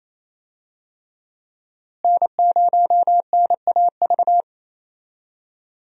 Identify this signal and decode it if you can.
Morse code, heard as a keyed tone, about 28 wpm, 700 Hz. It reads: N0NAV
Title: USA Call Signs - 5 Characters - Repeated 3x - 28wpm